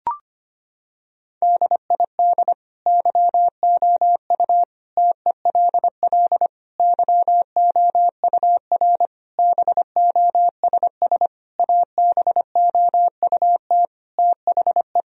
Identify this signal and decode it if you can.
Morse code, heard as a keyed tone, about 25 wpm, 700 Hz. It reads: DID YOU TELL YOUR BOSS ABOUT THE
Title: Sentences from Top 1000 Words - Review Part 4 - 25wpm